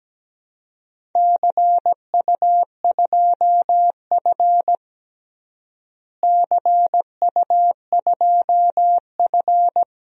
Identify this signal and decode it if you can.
Morse code, heard as a keyed tone, about 17 wpm, 700 Hz. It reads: CU2F CU2F
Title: DXCC-100 Call Signs - 4 Characters - Repeated 3x - 17wpm